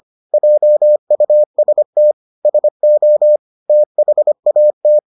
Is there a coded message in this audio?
Yes: Morse code from 0.3 to 5.0 s, heard as a keyed tone, about 25 wpm, 600 Hz. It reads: JUST SO THAT